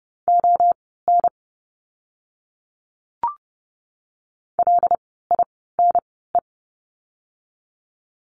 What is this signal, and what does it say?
Morse code, heard as a keyed tone, about 30 wpm, spaced as for 17 wpm, 700 Hz. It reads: ON LINE
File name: course / 30:10wpm / 115 Lesson 19f - Letter w - Words 30wpm